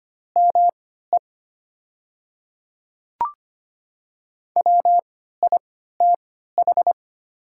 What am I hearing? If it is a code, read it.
Morse code, heard as a keyed tone, about 25 wpm, spaced as for 14 wpm, 700 Hz. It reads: ME WITH
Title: 3x Character Spacing - Top 100 Words - 25wpm